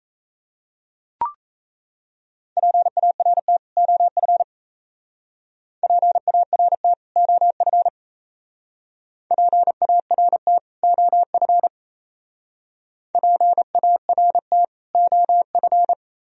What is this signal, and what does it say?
Morse code, heard as a keyed tone, about 36 wpm, 690 Hz. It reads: PART OF PART OF PART OF PART OF